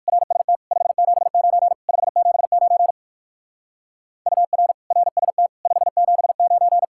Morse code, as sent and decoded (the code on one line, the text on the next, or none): .-. ... - / ..... --... ----. / ..... --... ----. / ..- .-. / .-. ... - / ..... --... ----.
RST 579 579 UR RST 579